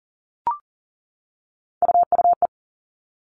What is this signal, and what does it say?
Morse code, heard as a keyed tone, about 40 wpm, 700 Hz. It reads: UUE
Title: All Cut Numbers - 3-Digit 40wpm